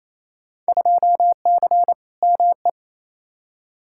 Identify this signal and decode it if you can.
Morse code, heard as a keyed tone, about 28 wpm, 700 Hz. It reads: 2C ME
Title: Field Day Exchanges Encore - 28wpm